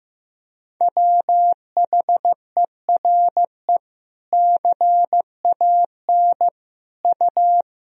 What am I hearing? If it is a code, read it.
Morse code, heard as a keyed tone, about 15 wpm, 700 Hz. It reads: WHERE CAN U
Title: Sentences from Top 500 Words and Abbreviations 15wpm